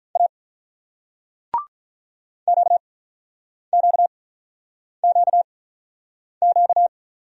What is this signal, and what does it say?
Morse code, heard as a keyed tone, about 48 wpm, spaced as for 9 wpm, 690 Hz. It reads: A QQQQ